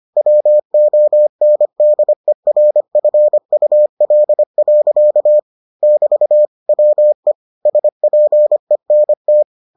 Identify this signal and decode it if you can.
Morse code, heard as a keyed tone, about 25 wpm, 600 Hz. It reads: WONDERFUL. <BT> WE SPENT